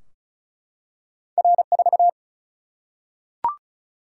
Morse code, heard as a keyed tone, about 36 wpm, 700 Hz. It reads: R4
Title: Single-Letter Single-Number 35wpm